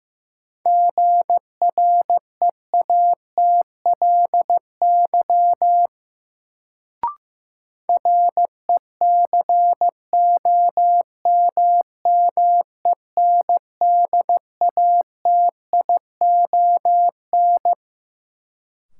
Morse code, heard as a keyed tone, about 15 wpm, 700 Hz. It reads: GREATLY RECOMMENDATION
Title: Sentences from Top 1400 Words - New Word Drill - 15wpm